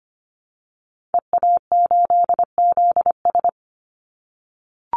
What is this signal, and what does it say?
Morse code, heard as a keyed tone, about 25 wpm, 700 Hz. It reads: EA8ZS
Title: Call Signs - 25wpm